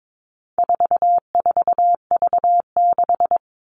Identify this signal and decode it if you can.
Morse code, heard as a keyed tone, about 22 wpm, 700 Hz. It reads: 44V6
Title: All Cut Numbers - 4-Digit 22wpm